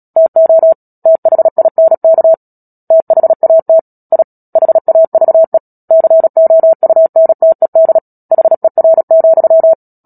Morse code, heard as a keyed tone, about 37 wpm, 650 Hz. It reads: TO THINK THAT I HAVE COUNTED HER,
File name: Book Two / 10-Chapter 10 - Epilogue - 36 WPM